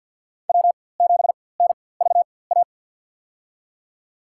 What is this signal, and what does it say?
Morse code, heard as a keyed tone, about 51 wpm, spaced as for 23 wpm, 700 Hz. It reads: W7NVA
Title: USA Call Signs - 4x Character Spacing - 50wpm